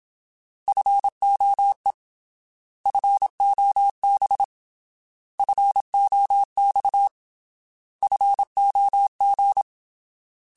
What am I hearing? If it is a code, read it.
Morse code, heard as a keyed tone, about 27 wpm, 800 Hz. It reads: FOE FOB FOX FOG
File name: Alphabetical / Alphabetical mp3s / Fs @25